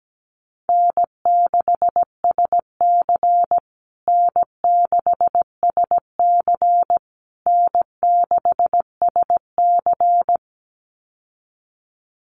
Morse code, heard as a keyed tone, about 17 wpm, 700 Hz. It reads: N6SC N6SC N6SC
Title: USA Call Signs Repeated 3x 17wpm